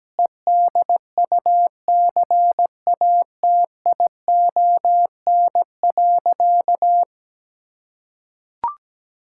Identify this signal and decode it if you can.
Morse code, heard as a keyed tone, about 17 wpm, 700 Hz. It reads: EDUCATION.
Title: Sentences from Top 1000 Words - Review Part 6 - 17wpm